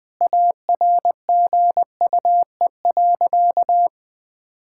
Morse code, heard as a keyed tone, about 19 wpm, 700 Hz. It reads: ARGUE.
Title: Sentences with 7-Word Phrases - Spoken Beginning - 20wpm